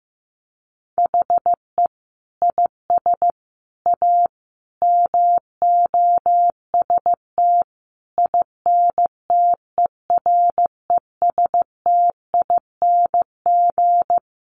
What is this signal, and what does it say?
Morse code, heard as a keyed tone, about 15 wpm, 700 Hz. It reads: HE IS A MOST INTERESTING
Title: Sentences from Top 1000 Words 15wpm